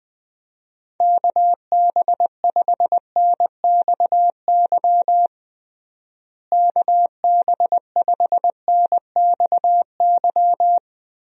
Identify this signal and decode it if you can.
Morse code, heard as a keyed tone, about 19 wpm, 700 Hz. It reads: KB5NXY KB5NXY
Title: USA Call Signs - 6 Characters - Repeated 3x - 20wpm